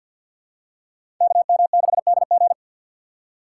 Morse code, heard as a keyed tone, about 48 wpm, 700 Hz. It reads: KM6DG
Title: USA Call Signs - 4x Character Spacing - 50wpm